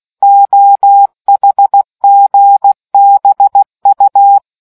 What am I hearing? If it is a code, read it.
Morse code, heard as a keyed tone, about 16 wpm, 800 Hz. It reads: OHGBU